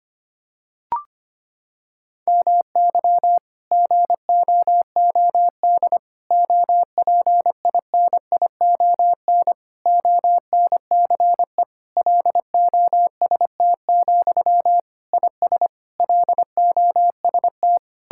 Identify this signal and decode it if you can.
Morse code, heard as a keyed tone, about 25 wpm, 700 Hz. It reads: MY GOOD OPINION ONCE LOST, IS LOST
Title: Sentences from Top 1600 Words 25wpm